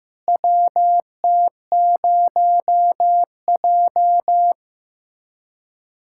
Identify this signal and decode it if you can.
Morse code, heard as a keyed tone, about 15 wpm, 700 Hz. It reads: WT0J
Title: USA Callsign DE Callsign - 4 Characters - 15wpm